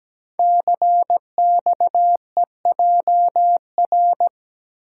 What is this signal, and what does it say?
Morse code, heard as a keyed tone, about 17 wpm, 700 Hz. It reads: CXEJR
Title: Code Groups - 5 Letter:Numbers - 17wpm